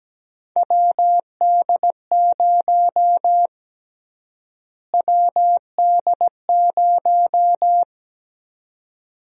Morse code, heard as a keyed tone, about 17 wpm, 700 Hz. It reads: WD0 WD0
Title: USA Call Sign Prefixes - Repeated 2x - 17wpm